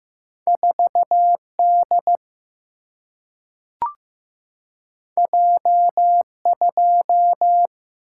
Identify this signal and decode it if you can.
Morse code, heard as a keyed tone, about 15 wpm, 700 Hz. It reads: 4D J2
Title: Single-Letter Single-Number 15wpm